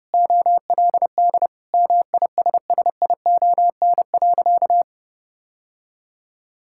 Morse code, heard as a keyed tone, about 30 wpm, 700 Hz. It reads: OLD MISSION.
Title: Sentences from Top 1600 Words 30wpm